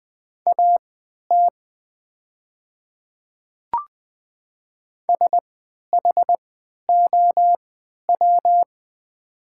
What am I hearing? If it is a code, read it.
Morse code, heard as a keyed tone, about 20 wpm, spaced as for 11 wpm, 700 Hz. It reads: AT SHOW
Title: 3x Character Spacing - Top 500 Words - 20wpm